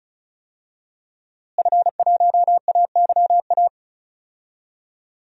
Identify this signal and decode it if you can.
Morse code, heard as a keyed tone, about 36 wpm, 700 Hz. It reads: F1AYA
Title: DXCC-100 Call Signs - 5 Characters - Repeated 2x - 35wpm